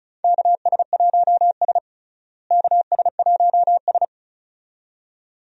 Morse code, heard as a keyed tone, about 34 wpm, 700 Hz. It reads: KS1S KS1S
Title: USA Call Signs - 4 Characters - Repeated 2x - 35wpm